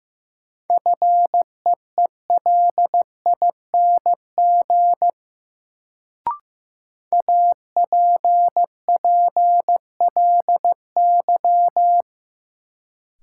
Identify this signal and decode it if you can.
Morse code, heard as a keyed tone, about 15 wpm, 700 Hz. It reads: FEELING APPLY
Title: Sentences from Top 1200 Words - New Word Drill Alt - 15wpm